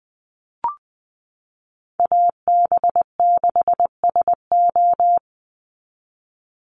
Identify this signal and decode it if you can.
Morse code, heard as a keyed tone, about 20 wpm, 690 Hz. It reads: AB6SO
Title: NA SOTA Chaser Callsigns 20wpm